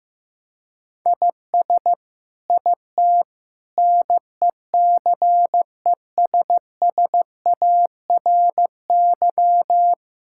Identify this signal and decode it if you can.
Morse code, heard as a keyed tone, about 15 wpm, 700 Hz. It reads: IS IT NECESSARY
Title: Sentences from Top 1000 Words - Review Part 6 - 15wpm